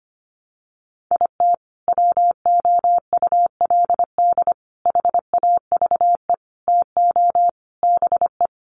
Morse code, heard as a keyed tone, about 25 wpm, 700 Hz. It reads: IT WOULD HAVE TO BE
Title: Sets of 5 Words Encore 25wpm